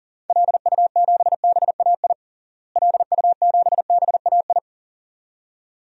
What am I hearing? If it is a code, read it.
Morse code, heard as a keyed tone, about 40 wpm, 700 Hz. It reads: LU7BAI LU7BAI
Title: DXCC-100 Call Signs - 6 Characters - Repeated 2x - 40wpm